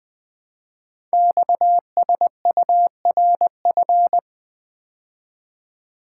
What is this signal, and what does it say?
Morse code, heard as a keyed tone, about 20 wpm, 700 Hz. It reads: XSURF
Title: Code Groups - 5 Letters - 20wpm